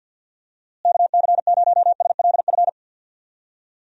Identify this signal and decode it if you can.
Morse code, heard as a keyed tone, about 49 wpm, 700 Hz. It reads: KC0SLF
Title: USA Call Signs - 6 Characters - Repeated 3x - 50wpm